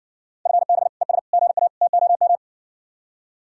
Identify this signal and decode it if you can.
Morse code, heard as a keyed tone, about 60 wpm, 700 Hz. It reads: FB ES GA TOM